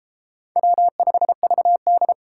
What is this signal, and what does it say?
Morse code, heard as a keyed tone, about 32 wpm, 700 Hz. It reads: W5VD